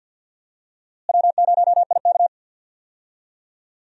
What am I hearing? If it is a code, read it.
Morse code, heard as a keyed tone, about 47 wpm, 700 Hz. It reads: W0IK
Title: USA Call Signs - 4 Characters - Repeated 3x - 50wpm